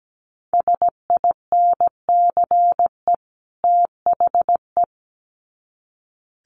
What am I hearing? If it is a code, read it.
Morse code, heard as a keyed tone, about 17 wpm, 700 Hz. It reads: SINCE THE
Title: Sets of 2 Words 17wpm